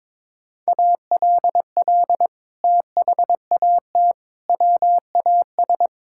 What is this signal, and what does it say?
Morse code, heard as a keyed tone, about 21 wpm, 700 Hz. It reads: ALL THAT WAS
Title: Sentences from Top 1000 Words - Review Part 5 - 22wpm